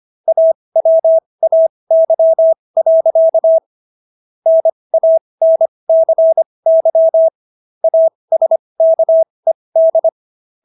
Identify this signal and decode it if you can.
Morse code, heard as a keyed tone, about 25 wpm, spaced as for 20 wpm, 650 Hz. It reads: AWAY. NANCY ASKED